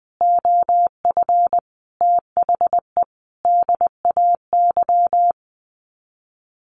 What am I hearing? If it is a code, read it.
Morse code, heard as a keyed tone, about 20 wpm, 700 Hz. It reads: OF THE DAY
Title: Sets of 5 Words Encore2 20wpm